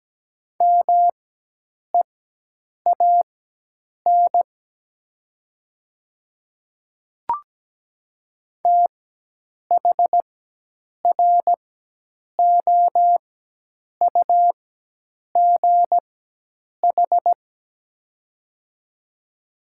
Morse code, heard as a keyed tone, about 17 wpm, spaced as for 8 wpm, 700 Hz. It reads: MEAN THROUGH
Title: 4x Character Spacing - Top 300 Words - 17wpm